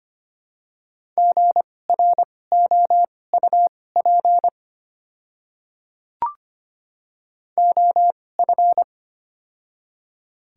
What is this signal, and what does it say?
Morse code, heard as a keyed tone, about 25 wpm, spaced as for 18 wpm, 700 Hz. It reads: GROUP OF
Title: 2x Character Spacing - Top 300 Words - 25wpm